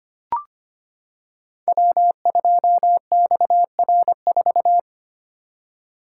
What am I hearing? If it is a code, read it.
Morse code, heard as a keyed tone, about 25 wpm, 690 Hz. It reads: W2XR4
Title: Code Groups - 5 Letter:Numbers - 25wpm